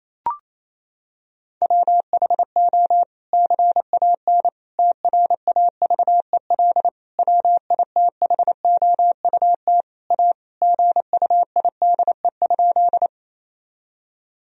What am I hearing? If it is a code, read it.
Morse code, heard as a keyed tone, about 28 wpm, 700 Hz. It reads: WHO CAN TRAVEL WITHOUT A GUIDE?
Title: Sentences from Top 1000 Words - Review Part 3 - 28wpm